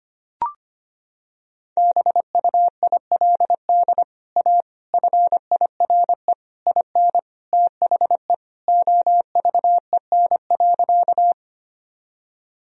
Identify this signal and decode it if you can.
Morse code, heard as a keyed tone, about 25 wpm, 700 Hz. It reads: BUILD A FIRE IN THE OVEN.
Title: Sentences from Top 1000 Words - Review Part 4 - 25wpm